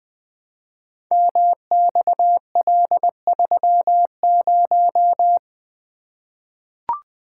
Morse code, heard as a keyed tone, about 20 wpm, 700 Hz. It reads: MXL30
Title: Code Groups - 5 Letter:Numbers - 20wpm